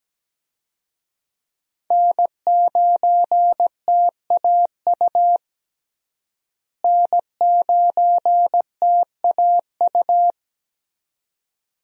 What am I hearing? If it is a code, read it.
Morse code, heard as a keyed tone, about 17 wpm, 700 Hz. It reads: N9TAU N9TAU
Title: USA Call Signs - 5 Characters - Repeated 2x - 17wpm